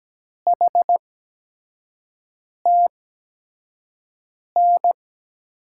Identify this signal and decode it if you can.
Morse code, heard as a keyed tone, about 17 wpm, spaced as for 5 wpm, 700 Hz. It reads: HTN
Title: USA Call Signs - 6 Characters - 8x Character Spacing - 17wpm